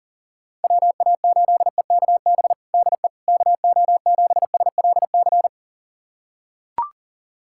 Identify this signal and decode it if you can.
Morse code, heard as a keyed tone, about 37 wpm, 700 Hz. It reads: WA8EKB DE KO7SLC
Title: USA Callsign DE Callsign - 6 Characters - 40wpm